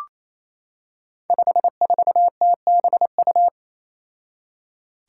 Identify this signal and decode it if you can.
Morse code, heard as a keyed tone, about 27 wpm, 700 Hz. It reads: 54TBU
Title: Five Challenging Letter-Number Combinations 28wpm